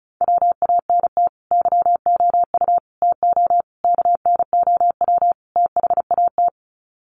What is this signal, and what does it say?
Morse code, heard as a keyed tone, about 34 wpm, 700 Hz. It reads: WANT YOU TO KNOW THAT
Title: Sets of 5 Words Encore 35wpm